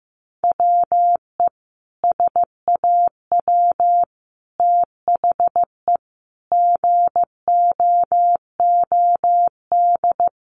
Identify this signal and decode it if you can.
Morse code, heard as a keyed tone, about 15 wpm, 700 Hz. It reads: WE SAW THE GOOD
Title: Sentences from Top 1100 Words 15wpm